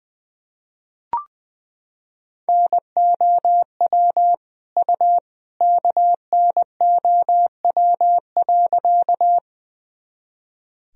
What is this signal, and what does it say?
Morse code, heard as a keyed tone, about 20 wpm, 700 Hz. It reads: NOW U KNOW.
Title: Sentences from Top 100 Words and Abbreviations 20wpm